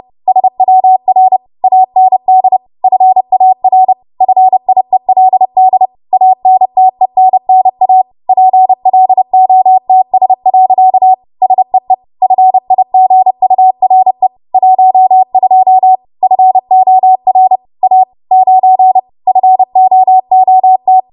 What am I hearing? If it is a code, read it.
Morse code, heard as a keyed tone, about 30 wpm, 750 Hz. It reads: SWR AND FAR FIELD ANTENNA PLOTS. SEE FIGURE 12 FOR A 9 FOOT
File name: Morse / 30WPM